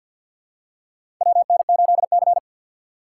Morse code, heard as a keyed tone, about 54 wpm, 700 Hz. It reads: WN8C
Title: USA Call Signs - 6x Character Spacing - 50wpm